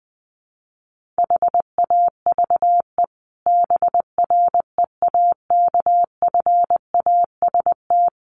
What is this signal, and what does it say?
Morse code, heard as a keyed tone, about 20 wpm, 700 Hz. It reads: HAVE BREAKFAST
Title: Sentences from Top 1500 Words 20wpm